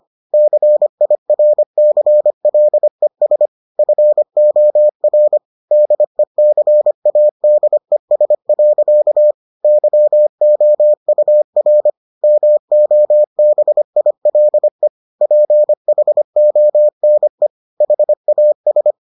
Morse code, heard as a keyed tone, about 25 wpm, 600 Hz. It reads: CIRCLES FOR DECADES. YOUR MOBILE PHONE HAS